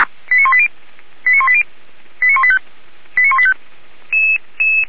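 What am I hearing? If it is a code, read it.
Five-tone selective call (ZVEI), heard as quick runs of five tones, tones of about 70 ms. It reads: 89180 89180 89185 89185
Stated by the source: ZVEI five-tone call received on an SDR, fire brigade.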